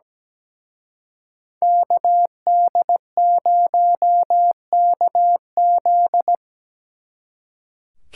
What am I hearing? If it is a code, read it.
Morse code, heard as a keyed tone, about 20 wpm, 700 Hz. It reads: KD0KZ
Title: Speed Racing - USA 2x2 Call Signs 17wpm